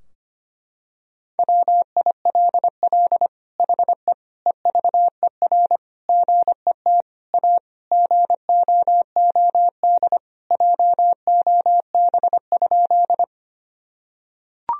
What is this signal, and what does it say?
Morse code, heard as a keyed tone, about 25 wpm, 700 Hz. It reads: WILL HE EVER GET A GOOD JOB?
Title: Sentences from Top 1000 Words - Review Part 3 - 25wpm